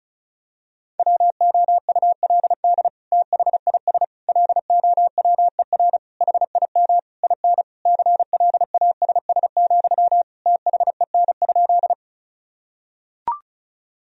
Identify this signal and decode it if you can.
Morse code, heard as a keyed tone, about 36 wpm, 700 Hz. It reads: WOULD THIS LOWER HIM IN CLASS, THEN?
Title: Sentences from Top 1000 Words - Review Part 4 - 35wpm